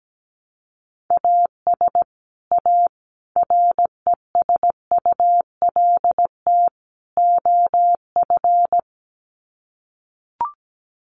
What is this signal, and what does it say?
Morse code, heard as a keyed tone, about 17 wpm, 700 Hz. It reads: AS A RESULT OF
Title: Sets of 4 Words 17wpm